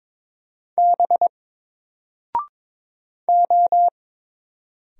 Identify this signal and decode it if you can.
Morse code, heard as a keyed tone, about 22 wpm, 700 Hz. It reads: B O
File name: Single-Letter-Number-22wpm